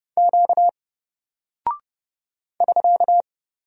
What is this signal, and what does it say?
Morse code, heard as a keyed tone, about 30 wpm, 700 Hz. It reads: Q <SK>
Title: Single Character (Letter:Number:Punctuation:Prosign) Alt 30wpm